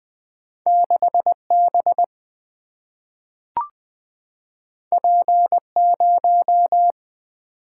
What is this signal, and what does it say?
Morse code, heard as a keyed tone, about 20 wpm, 700 Hz. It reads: 6 B P 0
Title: Single-Letter Single-Number 20wpm